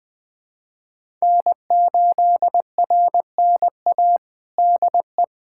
Morse code, heard as a keyed tone, about 20 wpm, 700 Hz. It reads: N8RNA DE
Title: USA Callsign DE Callsign 20wpm